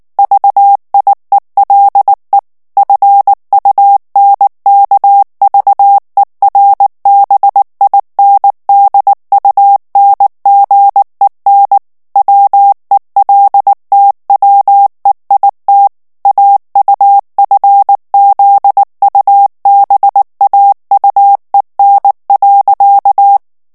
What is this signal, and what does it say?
Morse code, heard as a keyed tone, about 19 wpm, 800 Hz. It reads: VIELE FUNKVERBINDUNGEN WELTWEIT AUFZUBAUEN.
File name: morse mp3 / ALLEMAND texte clair 2 vitesse 18 mots minute